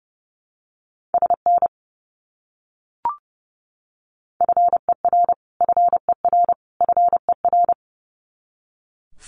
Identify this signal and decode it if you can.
Morse code, heard as a keyed tone, about 30 wpm, 700 Hz. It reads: SN FER FER FER
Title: Abbreviations Repeated 3x - 30wpm